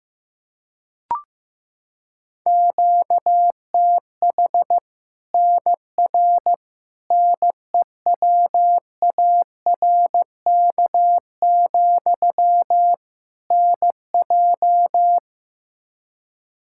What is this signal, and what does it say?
Morse code, heard as a keyed tone, about 15 wpm, 700 Hz. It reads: QTH NR NEWARK, NJ
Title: Practice QSO - Exchange 1 - Beginning - 15wpm